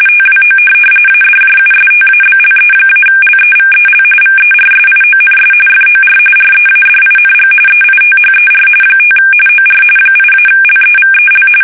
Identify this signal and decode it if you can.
RTTY, about 45.45 baud. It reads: TWO ADJUSTABLE-LEVEL VERSIONS FOR READING EITHER 5 OR 8-LEVEL
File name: fsk850 / teletype-corp